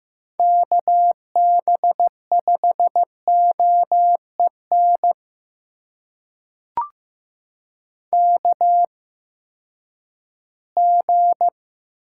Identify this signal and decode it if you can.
Morse code, heard as a keyed tone, about 15 wpm, spaced as for 10 wpm, 700 Hz. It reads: KB5OEN KG
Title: USA Call Signs - 8x Character Spacing - 15wpm